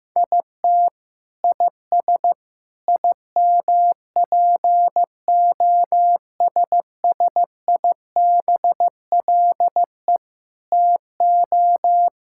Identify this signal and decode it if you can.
Morse code, heard as a keyed tone, about 15 wpm, 700 Hz. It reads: IT IS IMPOSSIBLE TO
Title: Sentences from Top 1600 Words 15wpm